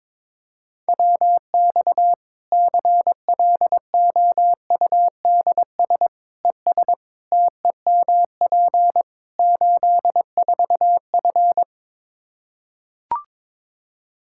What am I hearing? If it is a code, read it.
Morse code, heard as a keyed tone, about 22 wpm, 700 Hz. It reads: WX CLOUDS ES TEMP 84F
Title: Practice QSO - Exchange 2 - Info - 22wpm